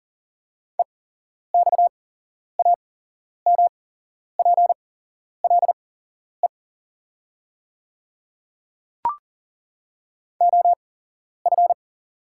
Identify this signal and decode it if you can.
Morse code, heard as a keyed tone, about 40 wpm, spaced as for 11 wpm, 690 Hz. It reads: EXAMPLE OF
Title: 8x Character Spacing - Top 300 Words - 40wpm